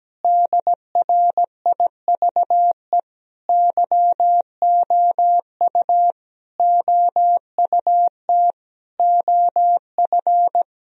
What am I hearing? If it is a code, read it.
Morse code, heard as a keyed tone, about 17 wpm, 700 Hz. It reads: DRIVE YOU OUT OF
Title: Sentences from Top 1000 Words - Review Part 2 - 17wpm